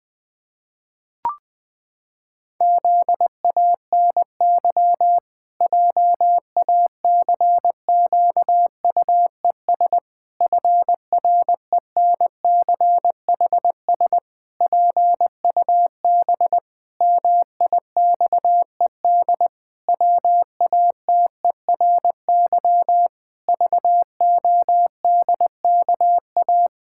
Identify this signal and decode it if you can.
Morse code, heard as a keyed tone, about 19 wpm, 700 Hz. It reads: ZANY JACQUES FRENCHS PUB MIXED WATERY VODKA
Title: Pangram Challenge v2 - 20wpm